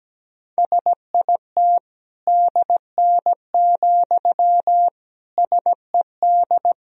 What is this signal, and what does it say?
Morse code, heard as a keyed tone, about 17 wpm, 700 Hz. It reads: SIT DN, SED
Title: Sentences from Top 500 Words and Abbreviations 17wpm